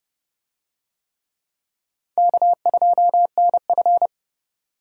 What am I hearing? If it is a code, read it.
Morse code, heard as a keyed tone, about 28 wpm, 700 Hz. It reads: K2NF